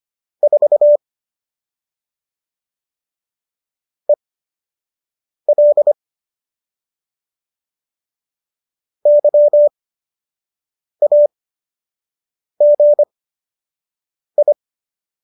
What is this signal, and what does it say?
Morse code, heard as a keyed tone, about 25 wpm, spaced as for 6 wpm, 590 Hz. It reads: 4 EL YAGI